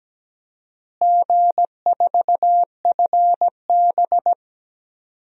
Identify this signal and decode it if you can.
Morse code, heard as a keyed tone, about 17 wpm, 700 Hz. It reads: G4FB
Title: DXCC-100 Call Signs - 4 Characters - Repeated 2x - 17wpm